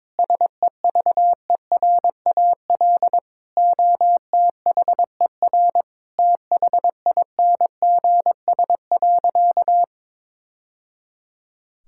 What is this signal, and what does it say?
Morse code, heard as a keyed tone, about 22 wpm, 700 Hz. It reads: SEVERAL OTHER THINGS.
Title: Sentences from Top 1000 Words - Review Part 6 - 22wpm